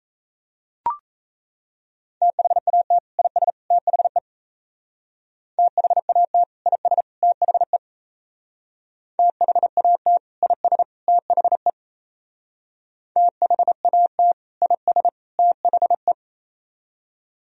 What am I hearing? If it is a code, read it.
Morse code, heard as a keyed tone, about 33 wpm, 700 Hz. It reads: THAT IS THE THAT IS THE THAT IS THE THAT IS THE
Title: Speed-Racing - Sets of 3 Words 28wpm